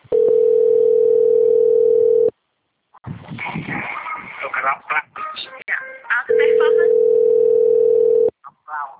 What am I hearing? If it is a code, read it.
Ringback tone (North America), 440 + 480 Hz.
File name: rg-1600-unknown- - - .50305